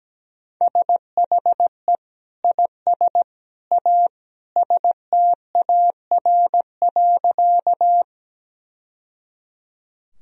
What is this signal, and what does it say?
Morse code, heard as a keyed tone, about 17 wpm, 700 Hz. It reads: SHE IS A STAR.
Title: Sentences from Top 1000 Words - Review Part 2 - 17wpm